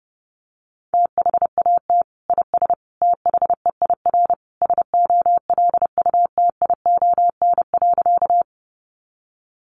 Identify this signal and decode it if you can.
Morse code, heard as a keyed tone, about 30 wpm, 700 Hz. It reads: THAT IS THEIR SOLUTION.
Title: Sentences from Top 1300 Words 30wpm